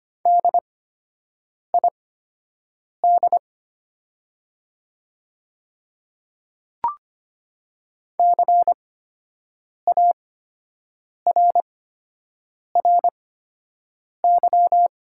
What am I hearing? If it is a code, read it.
Morse code, heard as a keyed tone, about 25 wpm, spaced as for 7 wpm, 700 Hz. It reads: DID CARRY
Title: 8x Character Spacing - Top 300 Words - 25wpm